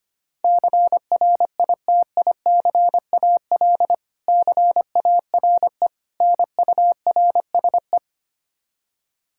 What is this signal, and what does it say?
Morse code, heard as a keyed tone, about 25 wpm, 700 Hz. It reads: CRITICAL CARE NURSE
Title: Practice QSO - Exchange 3 - More Info - 25wpm